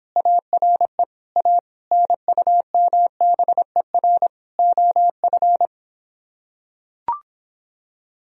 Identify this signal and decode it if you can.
Morse code, heard as a keyed tone, about 26 wpm, 700 Hz. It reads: ARE A NUMBER OF